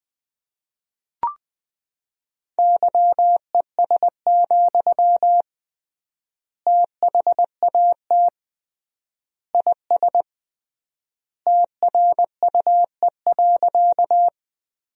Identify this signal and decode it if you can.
Morse code, heard as a keyed tone, about 20 wpm, 700 Hz. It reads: YES, THAT IS TRUE.
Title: 3x Word-Spacing - Sentences from Top 500 Words - 20wpm